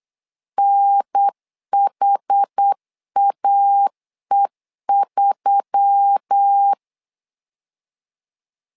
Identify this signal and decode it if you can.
Morse code, heard as a keyed tone, about 8 wpm, 800 Hz. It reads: NHAE3
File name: LEZIONI / 5 LBVC / 5 L 8wpm 2831